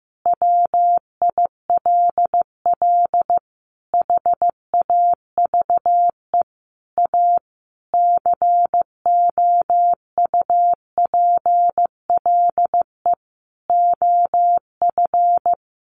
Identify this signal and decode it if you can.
Morse code, heard as a keyed tone, about 15 wpm, 700 Hz. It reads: WILL HAVE A COUPLE OF
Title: Sentences from Top 1100 Words 15wpm